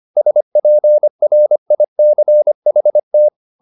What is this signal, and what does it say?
Morse code, heard as a keyed tone, about 25 wpm, 600 Hz. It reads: SPRICHT